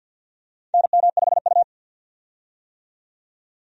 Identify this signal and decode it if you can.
Morse code, heard as a keyed tone, about 48 wpm, 700 Hz. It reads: NM5U
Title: USA Call Signs - 4 Characters - 50wpm